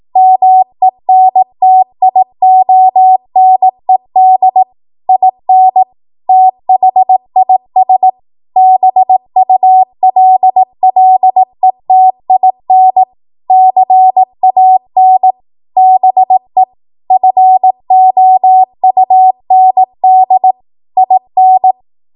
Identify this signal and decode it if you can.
Morse code, heard as a keyed tone, about 18 wpm, 750 Hz. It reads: MENTIONED IN THIS BULLETIN CAN BE FOUND IN
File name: Morse / 18WPM